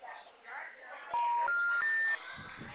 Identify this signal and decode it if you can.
Special information tone, with speech.